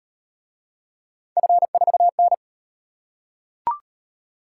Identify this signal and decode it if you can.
Morse code, heard as a keyed tone, about 35 wpm, 700 Hz. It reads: F4N